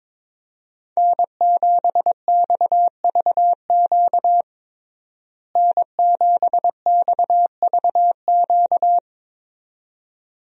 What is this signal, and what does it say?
Morse code, heard as a keyed tone, about 21 wpm, 700 Hz. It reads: N7XVQ N7XVQ
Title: USA Call Signs - 5 Characters - Repeated 2x - 22wpm